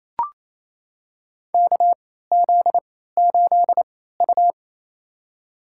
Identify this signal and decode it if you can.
Morse code, heard as a keyed tone, about 28 wpm, spaced as for 16 wpm, 700 Hz. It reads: KZ8U
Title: USA Call Signs - 4 Characters - 3x Character Spacing - 28wpm